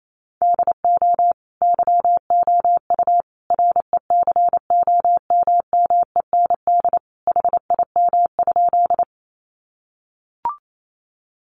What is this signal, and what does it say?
Morse code, heard as a keyed tone, about 28 wpm, 700 Hz. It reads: DO YOU RECOMMEND HIM?
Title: Sentences from Top 1200 Words 28wpm